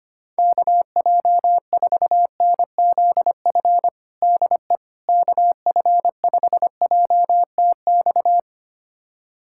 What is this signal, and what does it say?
Morse code, heard as a keyed tone, about 25 wpm, 700 Hz. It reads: KJ4NZF DE KF5JTX
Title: USA Callsign DE Callsign - 6 Characters - 25wpm